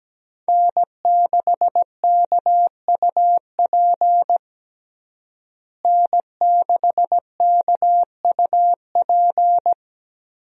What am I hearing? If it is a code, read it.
Morse code, heard as a keyed tone, about 17 wpm, 700 Hz. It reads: N6KUP N6KUP
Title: USA Call Signs - 5 Characters - Repeated 3x - 17wpm